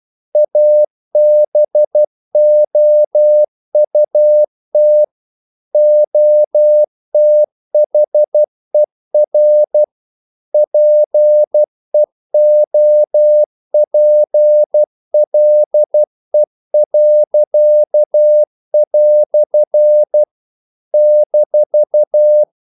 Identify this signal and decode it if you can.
Morse code, heard as a keyed tone, about 12 wpm, 600 Hz. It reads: ABOUT OTHER PEOPLE." -
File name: CW / Practice / EnglishQuotations0312